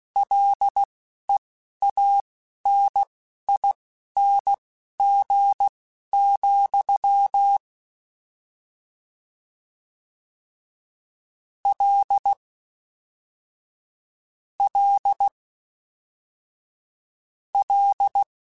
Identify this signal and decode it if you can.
Morse code, heard as a keyed tone, about 16 wpm, spaced as for 12 wpm, 800 Hz. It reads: LEANING, L L L